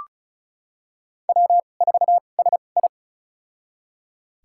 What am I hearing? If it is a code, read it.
Morse code, heard as a keyed tone, about 36 wpm, spaced as for 25 wpm, 700 Hz. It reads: W4SI
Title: USA Call Signs - 4 Characters - 2x Character Spacing - 35wpm